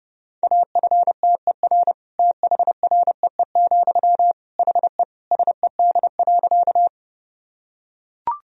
Morse code, heard as a keyed tone, about 30 wpm, 700 Hz. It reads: AFTER THREE, HE SED.
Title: Sentences from Top 200 Words and Abbreviations 30wpm